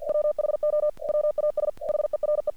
Morse code, heard as a keyed tone, about 48 wpm, 600 Hz. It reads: WHO WAS HERE
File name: archived / CRSnet